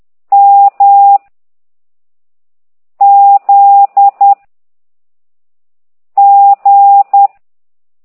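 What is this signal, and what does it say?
Morse code, heard as a keyed tone, about 10 wpm, 800 Hz. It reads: M Z G
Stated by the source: Morse code lesson 5 exercise 10 by CT2IDW, containing only characters M, O, G, Z, Q and comma.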